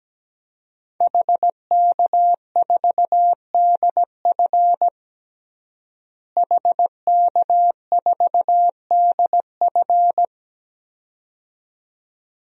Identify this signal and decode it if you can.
Morse code, heard as a keyed tone, about 17 wpm, 700 Hz. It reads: HK4DF HK4DF
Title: DXCC-100 Call Signs - 5 Characters - Repeated 2x - 17wpm